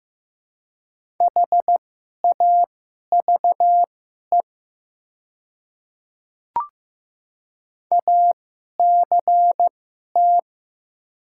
Morse code, heard as a keyed tone, about 15 wpm, spaced as for 11 wpm, 700 Hz. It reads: HAVE ACT
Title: 2x Character Spacing - Top 300 Words - 15wpm